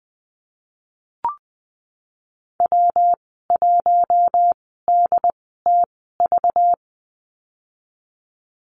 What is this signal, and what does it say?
Morse code, heard as a keyed tone, about 20 wpm, spaced as for 15 wpm, 700 Hz. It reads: W1DTV
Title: 193 Lesson 31g - Letter v - Call Signs 20wpm